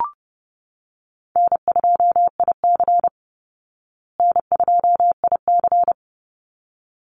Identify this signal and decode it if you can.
Morse code, heard as a keyed tone, about 30 wpm, 700 Hz. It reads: N2IC N2IC
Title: Easy Call Signs Repeated 2x - 30wpm